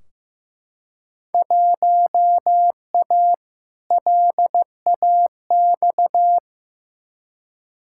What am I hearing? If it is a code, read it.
Morse code, heard as a keyed tone, about 15 wpm, 700 Hz. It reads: 1A LAX
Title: Field Day Exchanges Encore - 15wpm